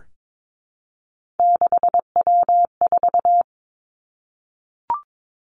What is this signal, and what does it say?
Morse code, heard as a keyed tone, about 22 wpm, 700 Hz. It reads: 6W4
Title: Three Challenging Letter-Number Combinations 22wpm